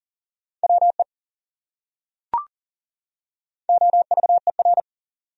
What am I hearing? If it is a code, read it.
Morse code, heard as a keyed tone, about 40 wpm, 700 Hz. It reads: WE OVER
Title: Top 100 Words 40wpm